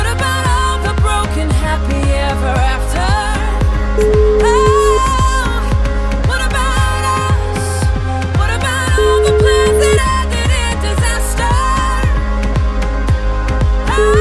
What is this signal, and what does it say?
Ringback tone (Europe, 425 Hz), with noise.